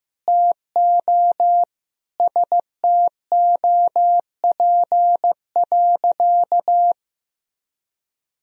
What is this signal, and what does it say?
Morse code, heard as a keyed tone, about 15 wpm, 700 Hz. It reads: TO STOP.
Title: Sentences from Top 1000 Words - Review Part 2 - 15wpm